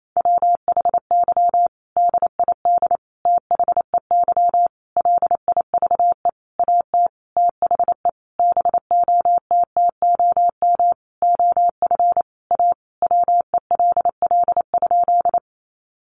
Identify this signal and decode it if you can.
Morse code, heard as a keyed tone, about 28 wpm, 700 Hz. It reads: WHY DID THEY LIVE AT THE BOTTOM OF A WELL?
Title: Sentences from Top 900 Words 28wpm